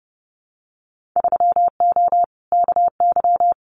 Morse code, heard as a keyed tone, about 30 wpm, 700 Hz. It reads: 3O KY
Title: Winter Field Day Exchanges 30wpm